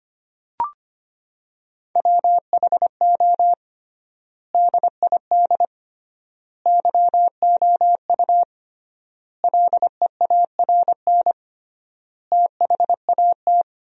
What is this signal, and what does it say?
Morse code, heard as a keyed tone, about 25 wpm, 700 Hz. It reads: WHO DID YOU LEARN THAT
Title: 3x Word-Spacing - Sentences from Top 300 Words - 25wpm